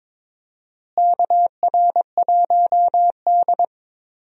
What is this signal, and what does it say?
Morse code, heard as a keyed tone, about 22 wpm, 700 Hz. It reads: KR1D
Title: USA Call Signs - 4 Characters - Repeated 3x - 22wpm